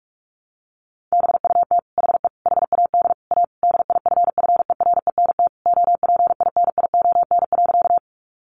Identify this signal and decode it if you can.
Morse code, heard as a keyed tone, about 44 wpm, 700 Hz. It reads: BUT HE HAD A DIFFERENT OPINION.
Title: Sentences from Top 1200 Words 45wpm